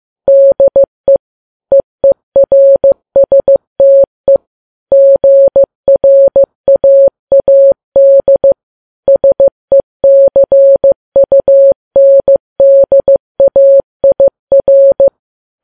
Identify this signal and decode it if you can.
Morse code, heard as a keyed tone, about 15 wpm, 550 Hz. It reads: DE EERSTE GRAAD SECUNDAIR
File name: Files Web tls / Morse on9fv / morse15kps